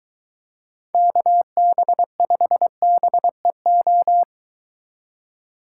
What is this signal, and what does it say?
Morse code, heard as a keyed tone, about 22 wpm, 700 Hz. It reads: KB5BEO